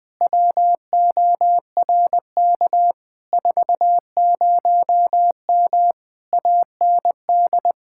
Morse code, heard as a keyed tone, about 20 wpm, 700 Hz. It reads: WORK 40M AND
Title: Speed-Racing Common QSO Phrases Encore - 20wpm